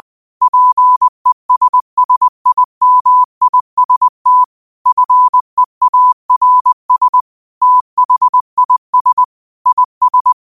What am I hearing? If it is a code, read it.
Morse code, heard as a keyed tone, about 20 wpm, 1000 Hz. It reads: PESSIMIST FEARS THIS IS
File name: qotdfiles / 20wpm